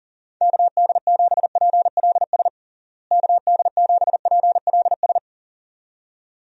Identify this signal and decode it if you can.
Morse code, heard as a keyed tone, about 40 wpm, 700 Hz. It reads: KD7PLS KD7PLS
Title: USA Call Signs - 6 Characters - Repeated 2x - 40wpm